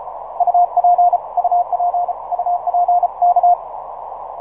Morse code, heard as a keyed tone, about 35 wpm, 710 Hz. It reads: UP UP UP K
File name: simple-file-list / 5A1AL 6MCW